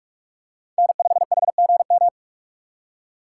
Morse code, heard as a keyed tone, about 44 wpm, 700 Hz. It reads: N5HGM
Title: USA Call Signs - 5 Characters - 3x Character Spacing - 45wpm